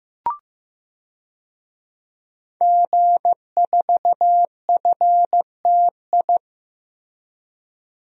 Morse code, heard as a keyed tone, about 15 wpm, 700 Hz. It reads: G4FTI
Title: DXCC-100 Call Signs - 5 Characters - Repeated 3x - 15wpm